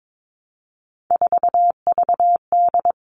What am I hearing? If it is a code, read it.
Morse code, heard as a keyed tone, about 22 wpm, 700 Hz. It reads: 4VD
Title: All Cut Numbers - 3-Digit 22wpm